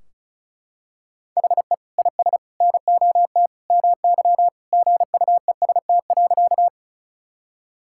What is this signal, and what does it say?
Morse code, heard as a keyed tone, about 36 wpm, 700 Hz. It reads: HE IS NOT MY GUEST.
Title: Sentences from Top 1600 Words 35wpm